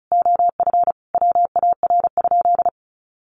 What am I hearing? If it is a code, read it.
Morse code, heard as a keyed tone, about 34 wpm, 700 Hz. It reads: OF WAR?
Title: Sentences from Top 1300 Words 35wpm